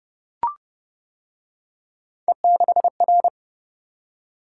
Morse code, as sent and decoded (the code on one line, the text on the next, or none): . -.... .-.
E6R